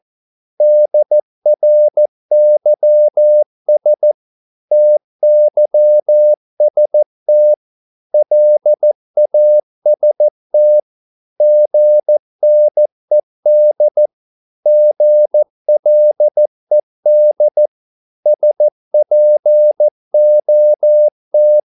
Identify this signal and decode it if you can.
Morse code, heard as a keyed tone, about 14 wpm, 600 Hz. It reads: DRYS TYST LAST GNED GLED SPOT
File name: Korte ord DK 14wpm